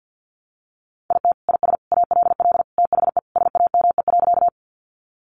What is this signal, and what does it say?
Morse code, heard as a keyed tone, about 50 wpm, 700 Hz. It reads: IT IS ALL THE SAME.
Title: Sentences from Top 200 Words 50wpm